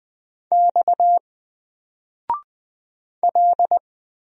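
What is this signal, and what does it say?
Morse code, heard as a keyed tone, about 20 wpm, 700 Hz. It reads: X L
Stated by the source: Single letter and number Morse drill at 20 wpm, from N4LCD.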